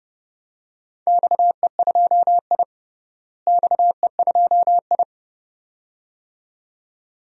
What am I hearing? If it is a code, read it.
Morse code, heard as a keyed tone, about 30 wpm, 700 Hz. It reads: XE2I XE2I
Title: DXCC-100 Call Signs - 4 Characters - Repeated 2x - 30wpm